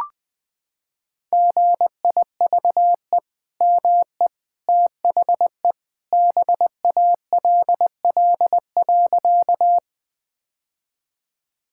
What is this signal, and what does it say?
Morse code, heard as a keyed tone, about 20 wpm, 700 Hz. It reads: GIVE ME THE BALL.
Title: Sentences from Top 1000 Words - Review Part 6 - 20wpm